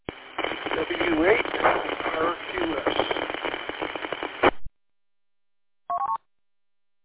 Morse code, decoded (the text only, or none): none